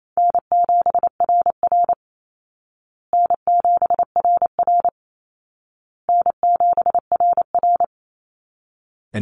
Morse code, heard as a keyed tone, about 28 wpm, 700 Hz. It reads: N7RR N7RR N7RR
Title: Easy Call Signs Repeated 3x - 28wpm